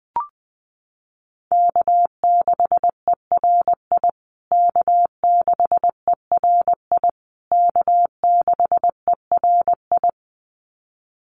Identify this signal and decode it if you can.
Morse code, heard as a keyed tone, about 20 wpm, 700 Hz. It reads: K6ERI K6ERI K6ERI
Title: USA Call Signs Repeated 3x 20wpm